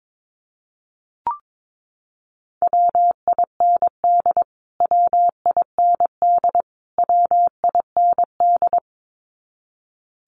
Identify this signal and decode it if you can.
Morse code, heard as a keyed tone, about 22 wpm, 700 Hz. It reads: WIND WIND WIND
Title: Top 500 Words Repeated 22wpm